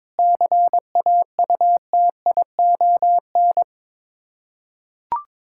Morse code, heard as a keyed tone, about 22 wpm, 700 Hz. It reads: CAUTION
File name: practice / 22wpm / Sets of 7 Words - 22wpm